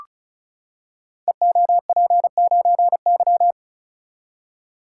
Morse code, heard as a keyed tone, about 36 wpm, 700 Hz. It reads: EOP9Y
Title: Code Groups - 5 Letter:Numbers - 35wpm